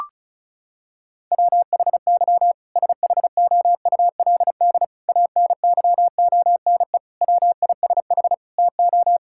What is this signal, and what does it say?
Morse code, heard as a keyed tone, about 36 wpm, 700 Hz. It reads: WHY SHOULD ANYONE WISH TO
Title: Sentences from Top 1600 Words 35wpm